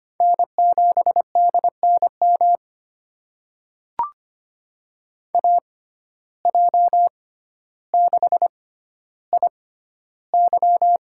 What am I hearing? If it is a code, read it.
Morse code, heard as a keyed tone, about 25 wpm, spaced as for 13 wpm, 700 Hz. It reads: N7DNM AJ6IY
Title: USA Call Signs - 5 Characters - 6x Character Spacing - 25wpm